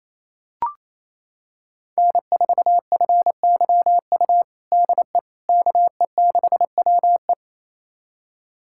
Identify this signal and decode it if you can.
Morse code, heard as a keyed tone, about 28 wpm, 700 Hz. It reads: N4FYU DE KE6WE
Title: USA Callsign DE Callsign - 5 Characters - 28wpm